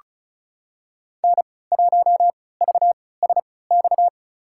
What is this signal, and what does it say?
Morse code, heard as a keyed tone, about 36 wpm, spaced as for 20 wpm, 690 Hz. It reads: N1VSX
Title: USA Call Signs - 5 Characters - 3x Character Spacing - 35wpm